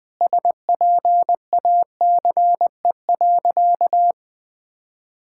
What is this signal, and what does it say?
Morse code, heard as a keyed tone, about 20 wpm, 700 Hz. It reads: SPACE.
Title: Sentences from Top 1000 Words - Review Part 5 - 20wpm